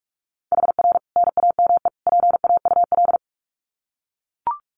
Morse code, heard as a keyed tone, about 44 wpm, 700 Hz. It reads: HR NAME PAUL
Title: Common QSO Phrases 45wpm